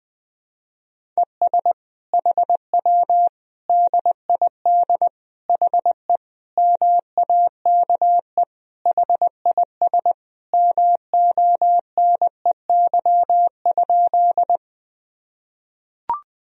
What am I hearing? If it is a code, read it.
Morse code, heard as a keyed tone, about 21 wpm, 700 Hz. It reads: ES HW DID HE MAKE HIS MONEY?
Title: Sentences from Top 400 Words and Abbreviations 20wpm